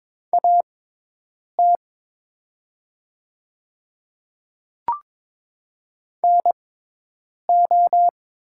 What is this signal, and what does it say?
Morse code, heard as a keyed tone, about 22 wpm, spaced as for 8 wpm, 700 Hz. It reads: AT NO
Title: 6x Character Spacing - Top 100 Words - 22wpm